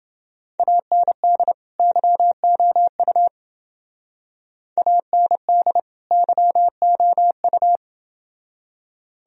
Morse code, heard as a keyed tone, about 29 wpm, 700 Hz. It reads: AND YOU AND YOU